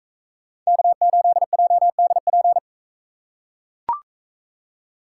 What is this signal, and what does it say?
Morse code, heard as a keyed tone, about 44 wpm, 700 Hz. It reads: K8JDP